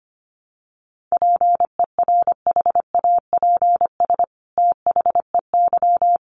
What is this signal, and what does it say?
Morse code, heard as a keyed tone, about 25 wpm, 700 Hz. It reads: PERHAPS THEY
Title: Sentences from Top 700 Words 25wpm